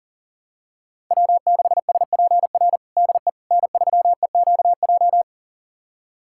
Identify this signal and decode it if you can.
Morse code, heard as a keyed tone, about 40 wpm, 700 Hz. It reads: W6SPR DE N3EQJ
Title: USA Callsign DE Callsign - 5 Characters - 40wpm